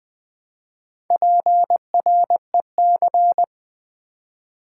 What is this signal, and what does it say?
Morse code, heard as a keyed tone, about 20 wpm, 700 Hz. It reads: PREC
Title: Build A Word - 2 Letters At A Time - 8-Letter Words - 20wpm